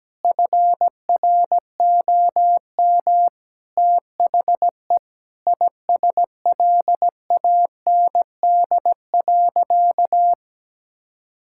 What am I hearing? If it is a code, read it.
Morse code, heard as a keyed tone, about 17 wpm, 700 Hz. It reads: FROM THE ISLAND.
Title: Sentences from Top 1000 Words - Review Part 5 - 17wpm